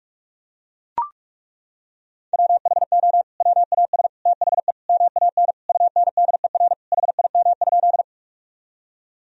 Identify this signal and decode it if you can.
Morse code, heard as a keyed tone, about 42 wpm, 700 Hz. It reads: WHO WAS THE MAN UNDER HIM?
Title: Sentences from Top 1000 Words - Review Part 1 - 45wpm